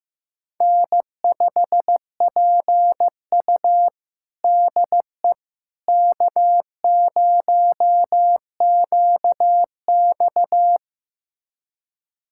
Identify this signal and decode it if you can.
Morse code, heard as a keyed tone, about 15 wpm, 700 Hz. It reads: N5PU DE K0QX
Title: USA Callsign DE Callsign - 4 Characters - 15wpm